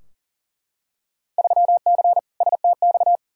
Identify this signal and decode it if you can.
Morse code, heard as a keyed tone, about 40 wpm, 700 Hz. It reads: 3C STX
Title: Field Day Exchanges Encore - 40wpm